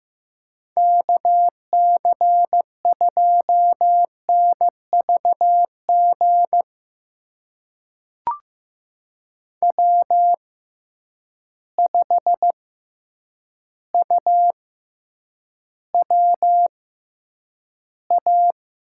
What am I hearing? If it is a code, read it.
Morse code, heard as a keyed tone, about 15 wpm, spaced as for 8 wpm, 700 Hz. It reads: KC2NVG W5UWA
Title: USA Call Signs - 6x Character Spacing - 15wpm